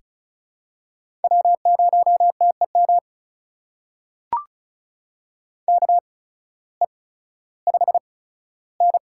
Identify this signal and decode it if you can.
Morse code, heard as a keyed tone, about 36 wpm, spaced as for 16 wpm, 700 Hz. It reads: W0TEM KE5N
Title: USA Call Signs - 8x Character Spacing - 35wpm